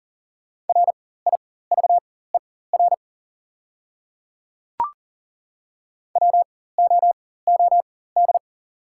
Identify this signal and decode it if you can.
Morse code, heard as a keyed tone, about 40 wpm, spaced as for 19 wpm, 690 Hz. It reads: RIVER WOOD
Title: 4x Character Spacing - Top 300 Words - 40wpm